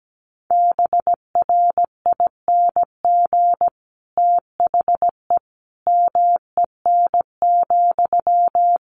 Morse code, heard as a keyed tone, about 17 wpm, 700 Hz. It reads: BRING THE MEN,
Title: Sentences from Top 1400 Words 17wpm